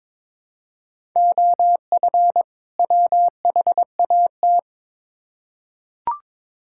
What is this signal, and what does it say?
Morse code, heard as a keyed tone, about 22 wpm, 700 Hz. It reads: OF WHAT
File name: Build A Phrase - 2-Word Phrases - 22wpm